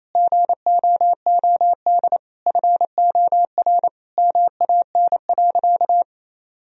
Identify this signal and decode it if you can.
Morse code, heard as a keyed tone, about 27 wpm, 700 Hz. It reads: GOOD FOR MAN.
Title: Sentences from Top 1000 Words - Review Part 3 - 28wpm